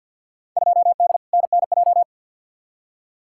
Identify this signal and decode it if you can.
Morse code, heard as a keyed tone, about 53 wpm, 700 Hz. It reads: 2D NNJ
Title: Field Day Exchanges - 50wpm